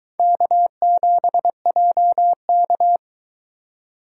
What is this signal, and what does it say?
Morse code, heard as a keyed tone, about 22 wpm, 700 Hz. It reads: K7JK